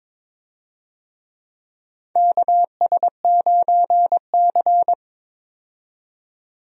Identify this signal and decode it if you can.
Morse code, heard as a keyed tone, about 22 wpm, 700 Hz. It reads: KS9C
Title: USA Call Signs - 4 Characters - 22wpm